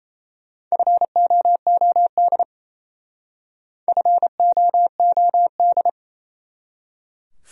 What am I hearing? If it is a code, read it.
Morse code, heard as a keyed tone, about 34 wpm, 700 Hz. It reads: FOOD FOOD
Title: Speed-Racing Top 300 Words 28wpm